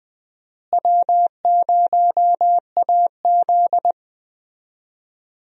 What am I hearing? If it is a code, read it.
Morse code, heard as a keyed tone, about 20 wpm, 700 Hz. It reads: W0AZ
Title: USA Call Signs - 4 Characters - Repeated 3x - 20wpm